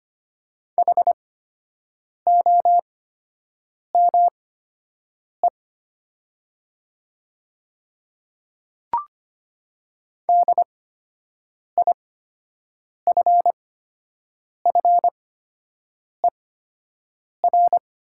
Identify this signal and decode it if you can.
Morse code, heard as a keyed tone, about 25 wpm, spaced as for 7 wpm, 700 Hz. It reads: HOME DIFFER
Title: 8x Character Spacing - Top 200 Words - 25wpm